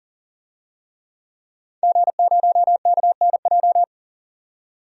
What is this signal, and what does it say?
Morse code, heard as a keyed tone, about 40 wpm, 700 Hz. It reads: G0KNJ
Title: DXCC-100 Call Signs - 5 Characters - 40wpm